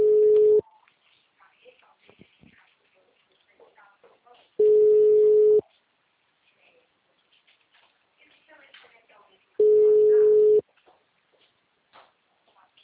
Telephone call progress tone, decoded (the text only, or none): ringback tone